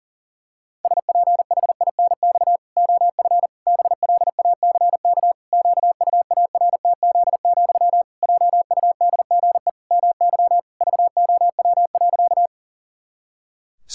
Morse code, heard as a keyed tone, about 40 wpm, 700 Hz. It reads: SPHINX OF BLACK QUARTZ, JUDGE MY VOW.
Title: Pangram Challenge v2 - 40wpm